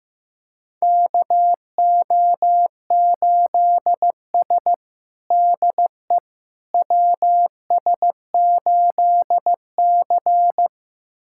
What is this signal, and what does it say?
Morse code, heard as a keyed tone, about 15 wpm, 700 Hz. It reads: KO8S DE WS8C
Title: USA Callsign DE Callsign - 4 Characters - 15wpm